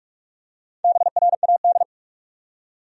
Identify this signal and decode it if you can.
Morse code, heard as a keyed tone, about 47 wpm, 700 Hz. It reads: BRAD
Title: USA Common Names - Max 4 Letters - 45wpm